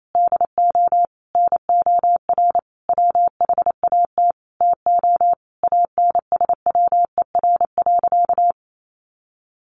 Morse code, heard as a keyed tone, about 28 wpm, 700 Hz. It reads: DO NOR WHAT TO ANSWER.
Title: Sentences from Top 1500 Words 28wpm